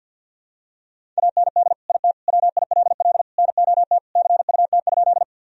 Morse code, heard as a keyed tone, about 49 wpm, 700 Hz. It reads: AND IT WILL NOT CUT?
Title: Sentences from Top 1000 Words - Review Part 1 - 50wpm